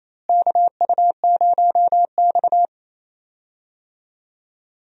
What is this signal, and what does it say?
Morse code, heard as a keyed tone, about 28 wpm, 700 Hz. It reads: KU0X
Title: USA Call Signs - 4 Characters - Repeated 2x - 28wpm